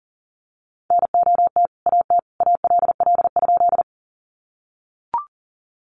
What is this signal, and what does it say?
Morse code, heard as a keyed tone, about 40 wpm, 700 Hz. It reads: NOT AT ALL?
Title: Sentences from Top 600 Words 40wpm